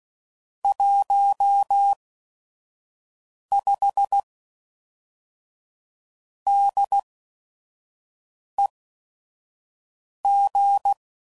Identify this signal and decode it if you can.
Morse code, heard as a keyed tone, about 16 wpm, spaced as for 5 wpm, 800 Hz. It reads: 15 DEG
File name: CW Code Groups @5WPM / QSOs / QSO WD0FBT